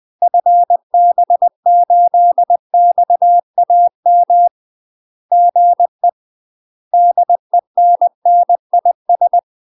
Morse code, heard as a keyed tone, about 20 wpm, 700 Hz. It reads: FB8XAM GE DENNIS
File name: UK MASTER QSOs TRAINING / 20WPM / UK MASTER QSOs 0028